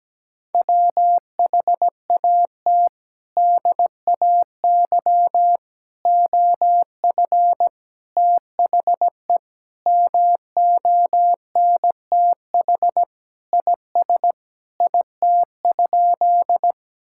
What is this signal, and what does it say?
Morse code, heard as a keyed tone, about 17 wpm, 700 Hz. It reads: WHAT DAY OF THE MONTH IS IT?
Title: Sentences from Top 1000 Words - Review Part 5 - 17wpm